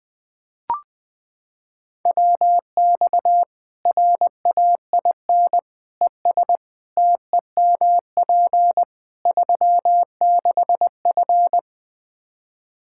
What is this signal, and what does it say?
Morse code, heard as a keyed tone, about 20 wpm, 700 Hz. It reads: WX RAIN ES TEMP 36F
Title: Practice QSO - Exchange 2 - Info - 20wpm